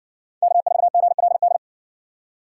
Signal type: Morse code

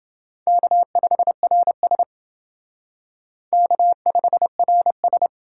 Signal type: Morse code

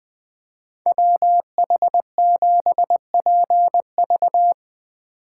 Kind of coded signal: Morse code

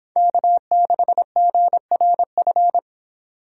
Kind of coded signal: Morse code

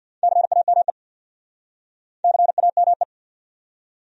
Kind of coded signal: Morse code

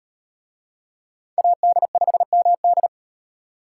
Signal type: Morse code

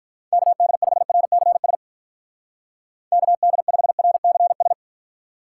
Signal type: Morse code

